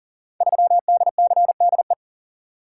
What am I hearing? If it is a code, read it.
Morse code, heard as a keyed tone, about 39 wpm, 700 Hz. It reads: 3DCDE
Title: Code Groups - 5 Letter:Numbers - 40wpm